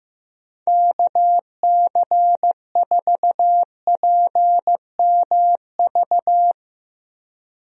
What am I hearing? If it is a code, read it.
Morse code, heard as a keyed tone, about 15 wpm, 700 Hz. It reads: KC4PMV
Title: USA Call Signs - 6 Characters - Repeated 3x - 15wpm